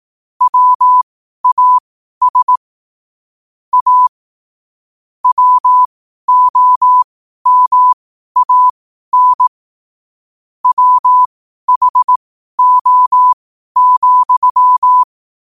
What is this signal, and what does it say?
Morse code, heard as a keyed tone, about 18 wpm, spaced as for 12 wpm, 1000 Hz. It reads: WAS A WOMAN WHO,
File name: 12wpm